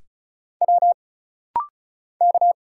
Morse code, heard as a keyed tone, about 34 wpm, 700 Hz. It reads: W K
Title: Single Letters - Rapid-Fire 35wpm